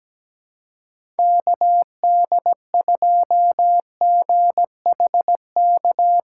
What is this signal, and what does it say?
Morse code, heard as a keyed tone, about 17 wpm, 700 Hz. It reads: KD2GHK
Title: USA Call Signs - 6 Characters - 4x Character Spacing - 17wpm